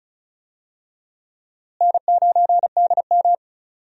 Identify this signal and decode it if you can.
Morse code, heard as a keyed tone, about 36 wpm, 700 Hz. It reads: N9DM
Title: USA Call Signs - 4 Characters - Repeated 2x - 35wpm